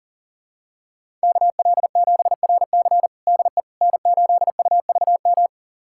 Morse code, heard as a keyed tone, about 37 wpm, 700 Hz. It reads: KL7RC DE N8UVM
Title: USA Callsign DE Callsign - 5 Characters - 40wpm